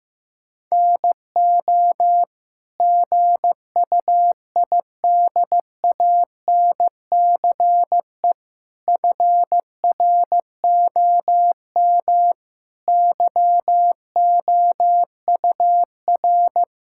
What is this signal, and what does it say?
Morse code, heard as a keyed tone, about 15 wpm, 700 Hz. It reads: NO GUIDANCE FROM YOUR
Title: Sentences from Top 1600 Words 15wpm